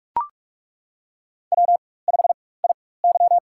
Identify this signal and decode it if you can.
Morse code, heard as a keyed tone, about 42 wpm, spaced as for 20 wpm, 700 Hz. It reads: W5IY
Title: USA Call Signs - 4x Character Spacing - 45wpm